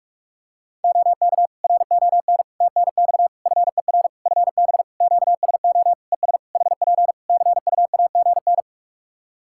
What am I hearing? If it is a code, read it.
Morse code, heard as a keyed tone, about 47 wpm, 700 Hz. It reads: OK RON TNX FER FB QSO ES HP CUAGN
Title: Practice QSO - Exchange 4 - Ending - 45wpm